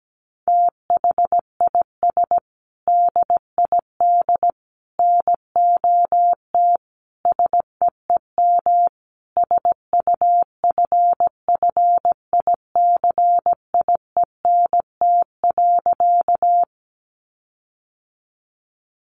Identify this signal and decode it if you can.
Morse code, heard as a keyed tone, about 17 wpm, 700 Hz. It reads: THIS DID NOT SEEM SUFFICIENT.
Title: Sentences from Top 1400 Words 17wpm